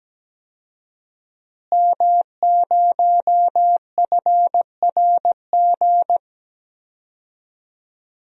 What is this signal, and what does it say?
Morse code, heard as a keyed tone, about 17 wpm, 700 Hz. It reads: M0FRG
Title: DXCC-100 Call Signs - 5 Characters - Repeated 2x - 17wpm